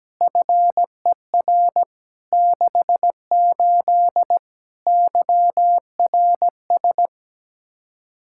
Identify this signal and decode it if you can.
Morse code, heard as a keyed tone, about 17 wpm, 700 Hz. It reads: FER 68 YRS
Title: Practice QSO - Exchange 3 - More Info - 17wpm